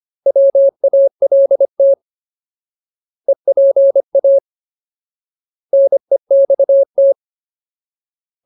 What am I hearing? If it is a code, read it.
Morse code, heard as a keyed tone, about 25 wpm, 550 Hz. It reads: WALT EPA NEXT
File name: Ultimate Word List / 25WPM Ultimate